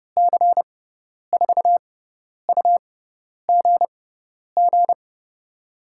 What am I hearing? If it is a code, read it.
Morse code, heard as a keyed tone, about 30 wpm, spaced as for 10 wpm, 700 Hz. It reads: C4UGG
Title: USA Call Signs - 6 Characters - 6x Character Spacing - 30wpm